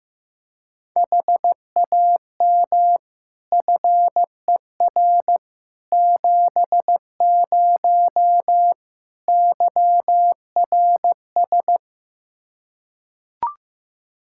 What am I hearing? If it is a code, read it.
Morse code, heard as a keyed tone, about 15 wpm, 700 Hz. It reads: HAM FER 70 YRS
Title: Practice QSO - Exchange 3 - More Info - 15wpm